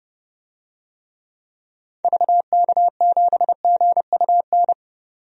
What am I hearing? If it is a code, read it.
Morse code, heard as a keyed tone, about 30 wpm, 700 Hz. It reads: VK7GUN
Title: DXCC-100 Call Signs - 6 Characters - 30wpm